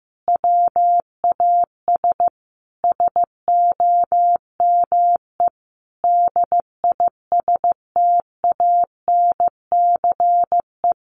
Morse code, heard as a keyed tone, about 15 wpm, 700 Hz. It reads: WAS SOME DISTANCE
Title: Sentences from Top 1400 Words 15wpm